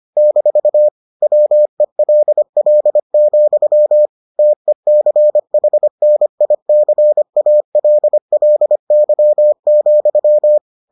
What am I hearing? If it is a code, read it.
Morse code, heard as a keyed tone, about 25 wpm, 600 Hz. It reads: - WELL, TECHNICALLY,